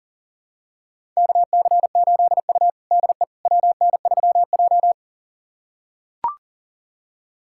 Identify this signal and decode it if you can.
Morse code, heard as a keyed tone, about 40 wpm, 700 Hz. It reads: KC8U DE WN3J
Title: USA Callsign DE Callsign - 4 Characters - 40wpm